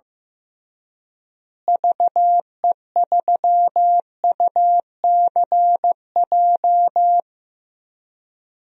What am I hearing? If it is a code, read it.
Morse code, heard as a keyed tone, about 15 wpm, 700 Hz. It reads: VE3UCJ
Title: DXCC-100 Call Signs - 6 Characters - Repeated 3x - 15wpm